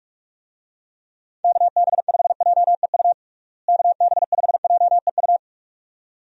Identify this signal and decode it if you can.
Morse code, heard as a keyed tone, about 42 wpm, 700 Hz. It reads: KB5JEU KB5JEU
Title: USA Call Signs - 6 Characters - Repeated 2x - 45wpm